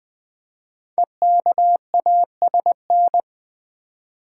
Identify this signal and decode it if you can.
Morse code, heard as a keyed tone, about 20 wpm, 700 Hz. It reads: EKASN
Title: Code Groups - 5 Letters - 20wpm